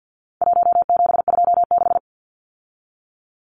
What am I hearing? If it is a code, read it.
Morse code, heard as a keyed tone, about 50 wpm, 700 Hz. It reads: 1726
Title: Numbers - 4-Digit 50wpm